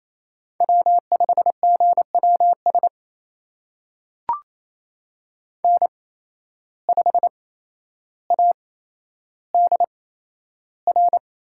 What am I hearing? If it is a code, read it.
Morse code, heard as a keyed tone, about 28 wpm, spaced as for 12 wpm, 700 Hz. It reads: W5GWS N5ADR
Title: USA Call Signs - 5 Characters - 8x Character Spacing - 28wpm